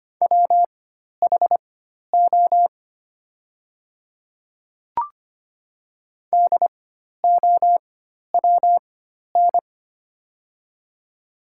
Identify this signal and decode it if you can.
Morse code, heard as a keyed tone, about 25 wpm, spaced as for 12 wpm, 700 Hz. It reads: WHO DOWN
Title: 4x Character Spacing - Top 200 Words - 25wpm